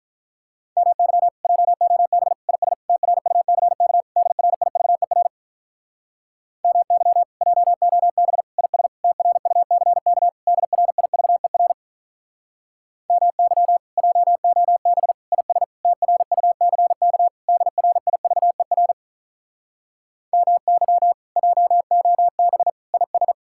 Morse code, heard as a keyed tone, about 47 wpm, 700 Hz. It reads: MY JOB IS TRUCK DRIVER MY JOB IS TRUCK DRIVER MY JOB IS TRUCK DRIVER MY JOB IS